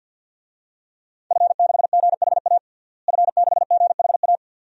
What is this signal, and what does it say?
Morse code, heard as a keyed tone, about 49 wpm, 700 Hz. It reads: F6GHA F6GHA
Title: DXCC-100 Call Signs - 5 Characters - Repeated 2x - 50wpm